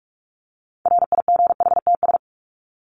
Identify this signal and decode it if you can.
Morse code, heard as a keyed tone, about 44 wpm, 700 Hz. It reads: RIGHTS
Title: ES:S Suffix - Top 100 Words with Base Words 45wpm